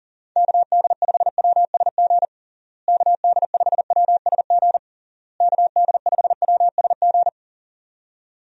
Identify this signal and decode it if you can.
Morse code, heard as a keyed tone, about 40 wpm, 700 Hz. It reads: KD5WSG KD5WSG KD5WSG
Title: USA Call Signs - 6 Characters - Repeated 3x - 40wpm